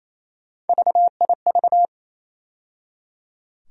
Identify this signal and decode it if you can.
Morse code, heard as a keyed tone, about 28 wpm, 700 Hz. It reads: VIV
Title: USA Call Sign Postfixes - 3 Letters 28wpm